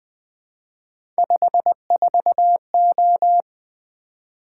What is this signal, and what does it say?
Morse code, heard as a keyed tone, about 20 wpm, 700 Hz. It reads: 54O
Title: Three Letter-Number Combinations - 20wpm